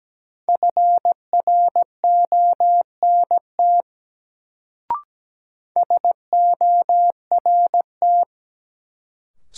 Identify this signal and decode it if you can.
Morse code, heard as a keyed tone, about 17 wpm, 700 Hz. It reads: FRONT SORT
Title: Sets of 2 Words - New Word Drill - 17wpm